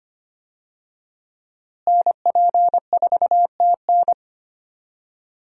Morse code, heard as a keyed tone, about 25 wpm, 700 Hz. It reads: NP4TN
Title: USA Call Signs - 5 Characters - Repeated 2x - 25wpm